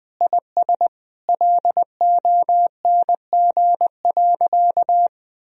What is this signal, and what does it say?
Morse code, heard as a keyed tone, about 19 wpm, 700 Hz. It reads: IS LONG.
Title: Sentences from Top 1000 Words - Review Part 4 - 20wpm